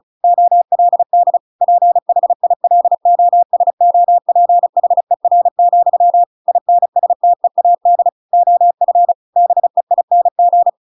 Morse code, heard as a keyed tone, about 35 wpm, 700 Hz. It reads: OLD PHILOSOPHER, INSTEAD OF BEING